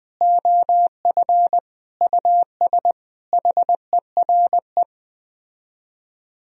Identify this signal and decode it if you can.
Morse code, heard as a keyed tone, about 20 wpm, 700 Hz. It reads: OF US HERE
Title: Speed-Racing - Sets of 5 Words 17wpm